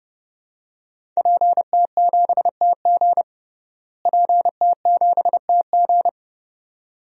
Morse code, heard as a keyed tone, about 32 wpm, 700 Hz. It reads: PT7TG PT7TG
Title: DXCC-100 Call Signs - 5 Characters - Repeated 2x - 30wpm